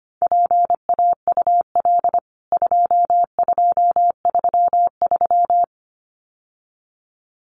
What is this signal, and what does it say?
Morse code, heard as a keyed tone, about 25 wpm, 700 Hz. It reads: PAUL 2233
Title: CWT Exchange 25wpm